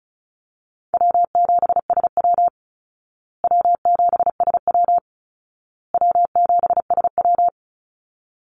Morse code, heard as a keyed tone, about 34 wpm, 700 Hz. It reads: W7SW W7SW W7SW
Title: Call Signs Repeated 2x - 35wpm